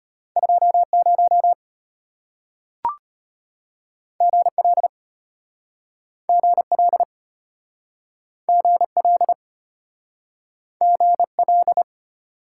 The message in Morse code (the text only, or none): 20 GL GL GL GL